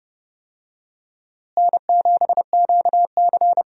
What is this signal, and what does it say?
Morse code, heard as a keyed tone, about 30 wpm, 700 Hz. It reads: N7QC
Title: USA Call Signs - 4 Characters - Repeated 2x - 30wpm